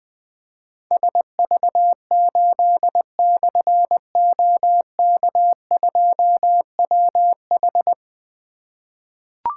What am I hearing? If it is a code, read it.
Morse code, heard as a keyed tone, about 20 wpm, 700 Hz. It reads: SV8/OK2WH
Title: Difficult Call Signs 20wpm